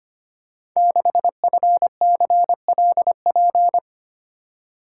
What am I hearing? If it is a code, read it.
Morse code, heard as a keyed tone, about 25 wpm, 700 Hz. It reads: 6FCLP
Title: Code Groups - 5 Letter:Numbers - 25wpm